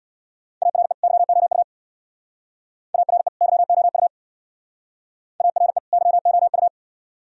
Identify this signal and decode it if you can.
Morse code, heard as a keyed tone, about 60 wpm, 700 Hz. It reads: ARE YOU ARE YOU ARE YOU
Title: Speed-Racing - Sets of 2 Words 50wpm